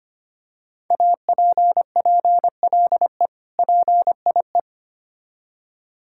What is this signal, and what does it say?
Morse code, heard as a keyed tone, about 25 wpm, 700 Hz. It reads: APPLE PIE
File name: practice / 25wpm / Difficult Words with Hints - 25wpm